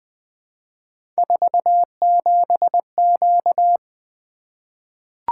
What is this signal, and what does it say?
Morse code, heard as a keyed tone, about 20 wpm, 700 Hz. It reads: 47Q
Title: Three Letter-Number Combinations - 20wpm